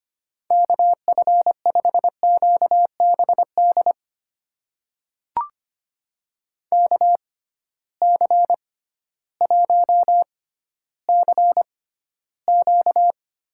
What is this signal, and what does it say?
Morse code, heard as a keyed tone, about 25 wpm, spaced as for 13 wpm, 700 Hz. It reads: KF5QBD KC1CQ
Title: USA Call Signs - 6x Character Spacing - 25wpm